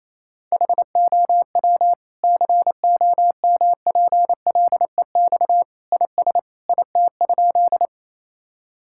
Morse code, heard as a keyed tone, about 28 wpm, 700 Hz. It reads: HOW COMPLEX IS IT?
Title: Sentences from Top 1000 Words - Review Part 6 - 28wpm